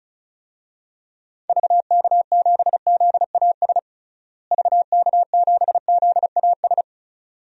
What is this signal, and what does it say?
Morse code, heard as a keyed tone, about 36 wpm, 700 Hz. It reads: VK7ZAS VK7ZAS
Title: DXCC-100 Call Signs - 6 Characters - Repeated 2x - 35wpm